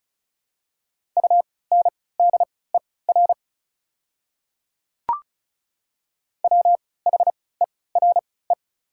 Morse code, heard as a keyed tone, about 34 wpm, spaced as for 20 wpm, 700 Hz. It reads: UNDER WHERE
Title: 3x Character Spacing - Top 200 Words - 35wpm